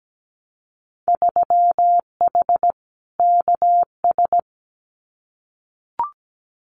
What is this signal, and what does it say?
Morse code, heard as a keyed tone, about 17 wpm, 700 Hz. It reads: 3H KS
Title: Winter Field Day Exchanges Encore 17wpm